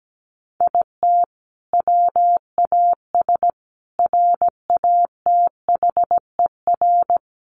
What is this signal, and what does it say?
Morse code, heard as a keyed tone, about 17 wpm, 700 Hz. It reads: IT WAS RATHER
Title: Sentences from Top 1400 Words 17wpm